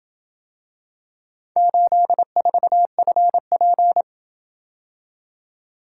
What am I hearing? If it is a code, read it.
Morse code, heard as a keyed tone, about 28 wpm, 700 Hz. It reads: 84FP